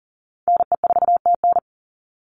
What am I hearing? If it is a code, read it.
Morse code, heard as a keyed tone, about 41 wpm, 700 Hz. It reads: NE4TN
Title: NA SOTA Chaser Callsigns 40wpm